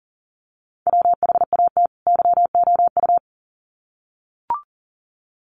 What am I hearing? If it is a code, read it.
Morse code, heard as a keyed tone, about 40 wpm, 700 Hz. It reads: WHAT YOU
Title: Sets of 2 Words 40wpm